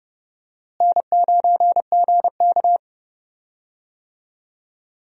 Morse code, heard as a keyed tone, about 31 wpm, 700 Hz. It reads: N9GK
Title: USA Call Signs - 4 Characters - Repeated 3x - 30wpm